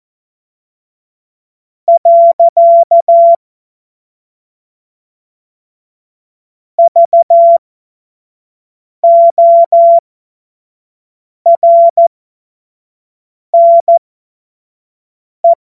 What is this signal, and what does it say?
Morse code, heard as a keyed tone, about 14 wpm, spaced as for 5 wpm, 680 Hz. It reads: . VORNE
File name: Morsekurs / Witze / Witz 30. August 2022 25 F70